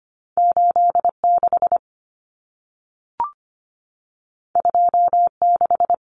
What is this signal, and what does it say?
Morse code, heard as a keyed tone, about 25 wpm, 700 Hz. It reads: 86 26
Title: Numbers - 2-Digit 25wpm